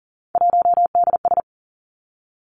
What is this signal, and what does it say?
Morse code, heard as a keyed tone, about 42 wpm, 700 Hz. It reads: 1DS
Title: Three Challenging Letter-Number Combinations 40wpm